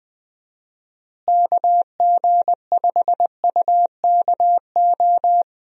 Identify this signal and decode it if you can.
Morse code, heard as a keyed tone, about 20 wpm, 700 Hz. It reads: KG5UKO
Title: USA Call Signs - 6 Characters - Repeated 3x - 20wpm